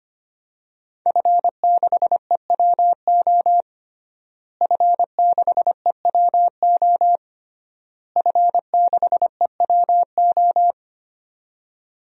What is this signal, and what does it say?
Morse code, heard as a keyed tone, about 25 wpm, 700 Hz. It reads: F6EWO F6EWO F6EWO
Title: DXCC-100 Call Signs - 5 Characters - Repeated 3x - 25wpm